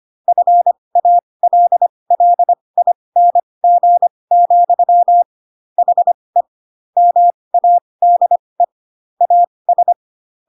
Morse code, heard as a keyed tone, about 25 wpm, spaced as for 20 wpm, 700 Hz. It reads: FALLING, HE MADE AS
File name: Pinocchio / Pinocchio ch7 15 25 ik1hge